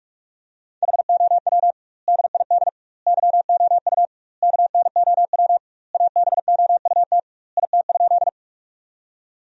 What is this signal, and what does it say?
Morse code, heard as a keyed tone, about 42 wpm, 700 Hz. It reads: HOW DID YOU KNOW ABOUT IT?
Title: 2x Word-Spacing - Sentences from Top 100 Words - 45wpm